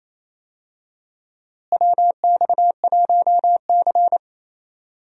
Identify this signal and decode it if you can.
Morse code, heard as a keyed tone, about 28 wpm, 700 Hz. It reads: WX1C
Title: USA Call Signs - 4 Characters - Repeated 2x - 28wpm